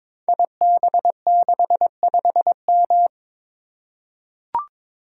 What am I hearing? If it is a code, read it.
Morse code, heard as a keyed tone, about 22 wpm, 690 Hz. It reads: IB65M
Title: Code Groups - 5 Letter:Numbers - 22wpm